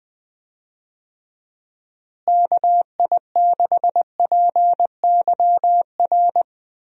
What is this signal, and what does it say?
Morse code, heard as a keyed tone, about 20 wpm, 700 Hz. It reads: KI6PYR
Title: USA Call Signs - 6 Characters - Repeated 2x - 20wpm